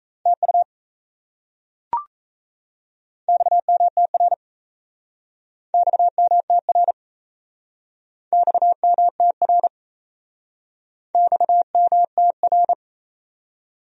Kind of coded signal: Morse code

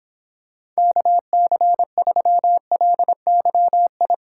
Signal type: Morse code

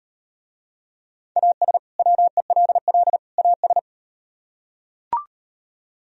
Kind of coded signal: Morse code

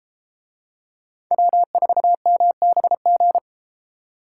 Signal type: Morse code

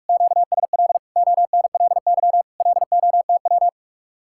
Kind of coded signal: Morse code

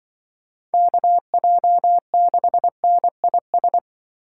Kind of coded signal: Morse code

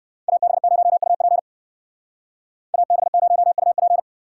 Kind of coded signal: Morse code

